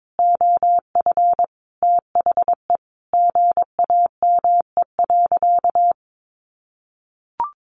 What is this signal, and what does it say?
Morse code, heard as a keyed tone, about 22 wpm, 700 Hz. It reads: OF THE GAME.
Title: Sentences from Top 1400 Words 22wpm